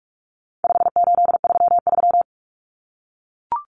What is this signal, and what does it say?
Morse code, heard as a keyed tone, about 44 wpm, 700 Hz. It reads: 5833
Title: Numbers - 4-Digit 45wpm